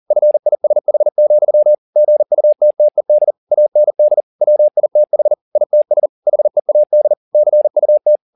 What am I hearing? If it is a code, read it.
Morse code, heard as a keyed tone, about 40 wpm, 600 Hz. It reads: FISH, GUTTED AND WITH ITS HEAD CUT
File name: the old man and the sea 2 40wpm